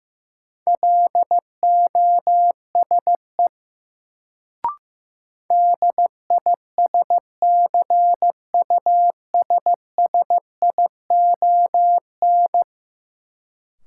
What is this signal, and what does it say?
Morse code, heard as a keyed tone, about 16 wpm, 700 Hz. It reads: LOSE DISCUSSION
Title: Sentences from Top 1100 Words - New Word Drill - 15wpm